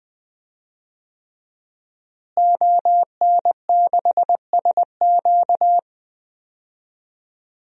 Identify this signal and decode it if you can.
Morse code, heard as a keyed tone, about 21 wpm, 700 Hz. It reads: ON6SQ
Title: DXCC-100 Call Signs - 5 Characters - Repeated 3x - 20wpm